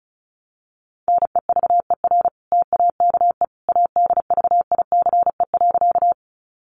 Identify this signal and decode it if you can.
Morse code, heard as a keyed tone, about 34 wpm, 700 Hz. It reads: NEVER TAKE ADVICE.
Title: Sentences from Top 1100 Words 35wpm